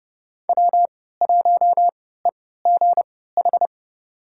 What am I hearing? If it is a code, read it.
Morse code, heard as a keyed tone, about 30 wpm, spaced as for 17 wpm, 700 Hz. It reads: W1EGH
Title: USA Call Signs - 5 Characters - 3x Character Spacing - 30wpm